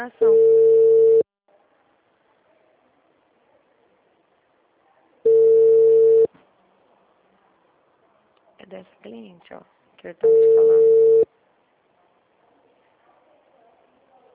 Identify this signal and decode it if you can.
Ringback tone (China, 450 Hz), with speech.